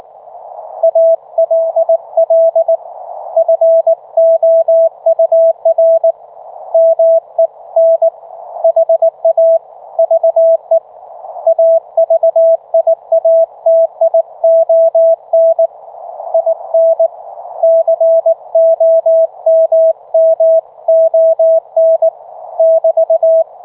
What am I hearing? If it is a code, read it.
Morse code, heard as a keyed tone, about 18 wpm, 660 Hz. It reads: ALL FOUR MEN HAVE AVIATION IN COMMON <BT>